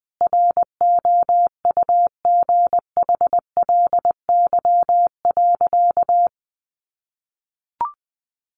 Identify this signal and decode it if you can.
Morse code, heard as a keyed tone, about 20 wpm, 700 Hz. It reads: ROUGHLY.
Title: Sentences from Top 1500 Words 20wpm